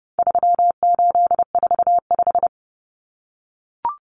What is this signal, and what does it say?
Morse code, heard as a keyed tone, about 30 wpm, 700 Hz. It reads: 3845
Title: Common Cut Numbers - 4-Digit 30wpm